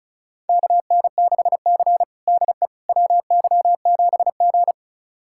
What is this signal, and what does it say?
Morse code, heard as a keyed tone, about 36 wpm, 700 Hz. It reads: KN6C DE WY7G
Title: USA Callsign DE Callsign - 4 Characters - 35wpm